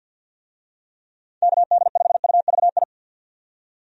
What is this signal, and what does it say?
Morse code, heard as a keyed tone, about 51 wpm, 700 Hz. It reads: KD5UVI
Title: USA Call Signs - 6 Characters - 50wpm